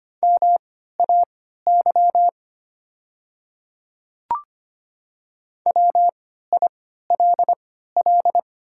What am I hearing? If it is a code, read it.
Morse code, heard as a keyed tone, about 25 wpm, spaced as for 14 wpm, 700 Hz. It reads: MAY WILL
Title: 3x Character Spacing - Top 200 Words - 25wpm